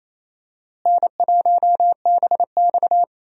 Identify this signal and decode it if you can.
Morse code, heard as a keyed tone, about 28 wpm, 700 Hz. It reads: N1BX
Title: USA Call Signs - 4 Characters - Repeated 3x - 28wpm